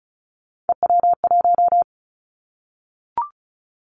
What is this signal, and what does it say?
Morse code, heard as a keyed tone, about 35 wpm, 700 Hz. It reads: EW1
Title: Callsign Prefixes for Top 50 DXCC - 35wpm